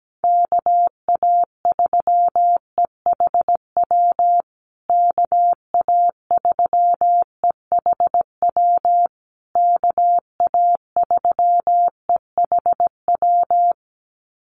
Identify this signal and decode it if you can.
Morse code, heard as a keyed tone, about 17 wpm, 700 Hz. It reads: KA3EHW KA3EHW KA3EHW
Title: USA Call Signs Repeated 3x 17wpm